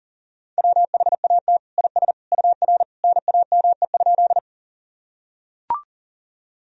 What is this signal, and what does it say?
Morse code, heard as a keyed tone, about 40 wpm, 700 Hz. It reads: WHAT IS UR NAME?
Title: Sentences from Top 200 Words and Abbreviations 40wpm